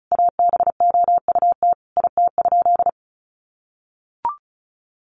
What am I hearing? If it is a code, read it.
Morse code, heard as a keyed tone, about 34 wpm, 700 Hz. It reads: ABOUT IT?
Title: Sentences from Top 200 Words 35wpm